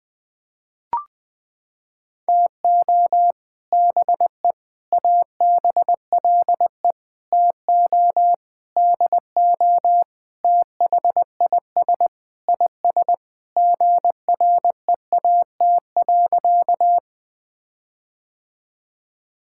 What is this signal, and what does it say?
Morse code, heard as a keyed tone, about 20 wpm, 700 Hz. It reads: TO BE ABLE TO DO THIS IS GREAT.
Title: Sentences with 5-Word Phrases - 20wpm